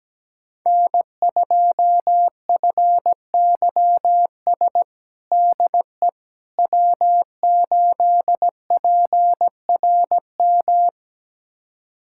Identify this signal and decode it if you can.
Morse code, heard as a keyed tone, about 17 wpm, 700 Hz. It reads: N2FYS DE W8PRM
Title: USA Callsign DE Callsign - 5 Characters - 17wpm